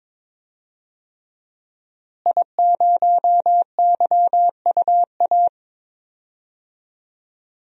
Morse code, heard as a keyed tone, about 22 wpm, 700 Hz. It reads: I0YUA
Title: DXCC-100 Call Signs - 5 Characters - Repeated 3x - 22wpm